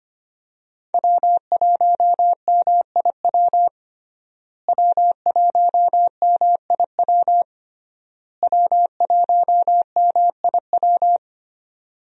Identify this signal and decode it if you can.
Morse code, heard as a keyed tone, about 25 wpm, 700 Hz. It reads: W1MIW W1MIW W1MIW
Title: USA Call Signs - 5 Characters - Repeated 3x - 25wpm